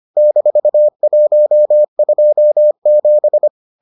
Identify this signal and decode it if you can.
Morse code, heard as a keyed tone, about 25 wpm, 600 Hz. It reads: -127